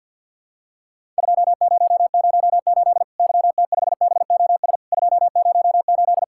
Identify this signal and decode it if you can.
Morse code, heard as a keyed tone, about 48 wpm, 700 Hz. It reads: 2008 YT5BOS 208
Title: Worked All Europe DX Contest - QTC Exchange - 50wpm